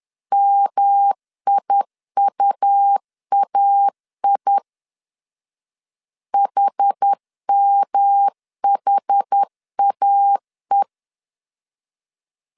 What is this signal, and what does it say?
Morse code, heard as a keyed tone, about 10 wpm, 800 Hz. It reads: MIUAI HMHAE
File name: CW / DET / 1 3 1500C S B 10wpm 3335